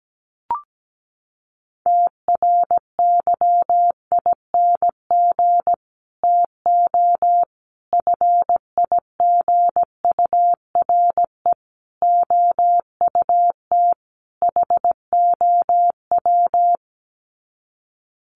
Morse code, heard as a keyed tone, about 17 wpm, 700 Hz. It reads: TRYING TO FIGURE OUT HOW
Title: Sets of 5 Words Encore 17wpm